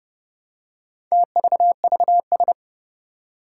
Morse code, heard as a keyed tone, about 30 wpm, 700 Hz. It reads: TVVS
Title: Four Challenging Letter-Number Combinations 30wpm